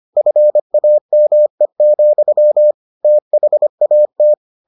Morse code, heard as a keyed tone, about 25 wpm, 600 Hz. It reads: FAME, THAT